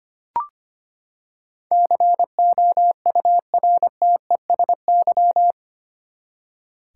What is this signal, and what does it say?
Morse code, heard as a keyed tone, about 25 wpm, 700 Hz. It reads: COURTESY
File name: Eight Letter Words - 25wpm